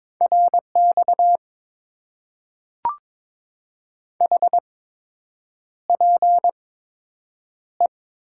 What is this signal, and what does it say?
Morse code, heard as a keyed tone, about 22 wpm, spaced as for 8 wpm, 700 Hz. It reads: RX HPE
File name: Top 90 QSO Elements - 8x Character Spacing - 22wpm